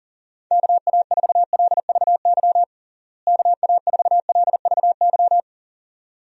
Morse code, heard as a keyed tone, about 40 wpm, 700 Hz. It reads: KA4LVY KA4LVY
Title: USA Call Signs - 6 Characters - Repeated 2x - 40wpm